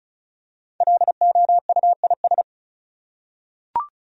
Morse code, heard as a keyed tone, about 35 wpm, 700 Hz. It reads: LOUIS
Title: USA Common Names - Max 5 Letters - 35wpm